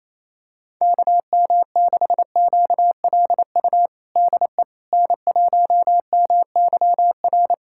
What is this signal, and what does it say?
Morse code, heard as a keyed tone, about 28 wpm, 700 Hz. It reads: KM6QLU DE N1MYR
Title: USA Callsign DE Callsign 28wpm